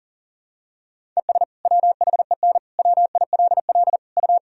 Morse code, heard as a keyed tone, about 37 wpm, 700 Hz. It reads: ES WHEN WILL U
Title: Sentences from Top 100 Words and Abbreviations 40wpm